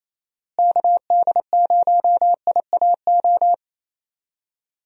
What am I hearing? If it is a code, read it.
Morse code, heard as a keyed tone, about 28 wpm, 700 Hz. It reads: KD0IAO
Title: USA Call Signs - 6 Characters - Repeated 3x - 28wpm